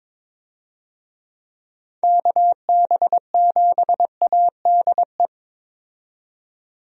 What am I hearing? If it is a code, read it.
Morse code, heard as a keyed tone, about 21 wpm, 700 Hz. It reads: KB7ADE
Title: USA Call Signs - 6 Characters - 22wpm